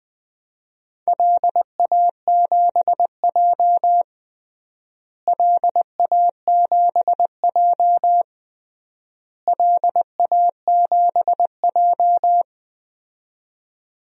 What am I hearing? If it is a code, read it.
Morse code, heard as a keyed tone, about 20 wpm, 700 Hz. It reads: LA7J LA7J LA7J
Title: DXCC-100 Call Signs - 4 Characters - Repeated 3x - 20wpm